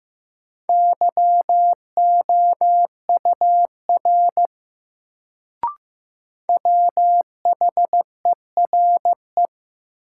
Morse code, heard as a keyed tone, about 15 wpm, 700 Hz. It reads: YOUR WHERE
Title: Sets of 2 Words - New Word Drill - 15wpm